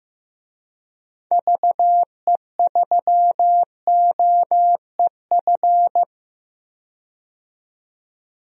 Morse code, heard as a keyed tone, about 15 wpm, 700 Hz. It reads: VE3OEF
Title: DXCC-100 Call Signs - 6 Characters - 15wpm